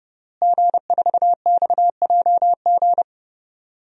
Morse code, heard as a keyed tone, about 32 wpm, 700 Hz. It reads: G4XJG
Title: DXCC-100 Call Signs - 5 Characters - Repeated 3x - 30wpm